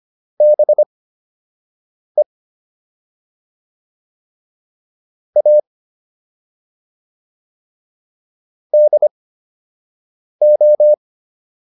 Morse code, heard as a keyed tone, about 25 wpm, spaced as for 6 wpm, 600 Hz. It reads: BE A DO